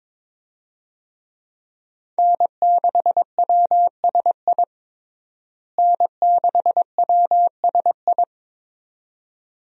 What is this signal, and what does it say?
Morse code, heard as a keyed tone, about 22 wpm, 700 Hz. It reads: N6WSI N6WSI
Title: USA Call Signs - 5 Characters - Repeated 2x - 22wpm